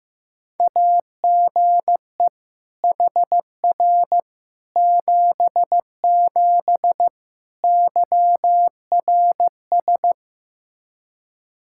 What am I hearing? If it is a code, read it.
Morse code, heard as a keyed tone, about 15 wpm, 700 Hz. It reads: AGE HR 77 YRS
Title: Practice QSO - Exchange 3 - More Info - 15wpm